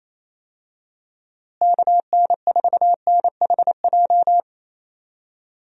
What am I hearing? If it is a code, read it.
Morse code, heard as a keyed tone, about 28 wpm, 700 Hz. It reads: KN4NHJ
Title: USA Call Signs - 6 Characters - Repeated 2x - 28wpm